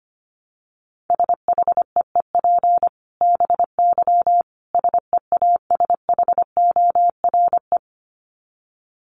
Morse code, heard as a keyed tone, about 25 wpm, 700 Hz. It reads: SHEEP BY SEASHORE
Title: Dit Challenge 25wpm